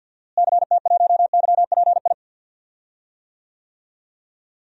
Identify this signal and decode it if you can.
Morse code, heard as a keyed tone, about 52 wpm, 700 Hz. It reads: CT1YPI
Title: DXCC-100 Call Signs - 6 Characters - Repeated 3x - 50wpm